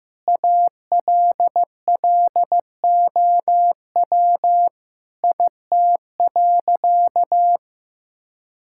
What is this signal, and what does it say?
Morse code, heard as a keyed tone, about 15 wpm, 700 Hz. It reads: ALLOW IT.
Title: Sentences from Top 1000 Words - Review Part 5 - 15wpm